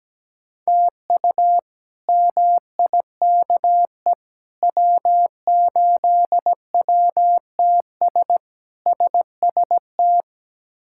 Morse code, heard as a keyed tone, about 17 wpm, 700 Hz. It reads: TU MIKE W8WTS SST
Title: K1USN SST QSO Practice - 17wpm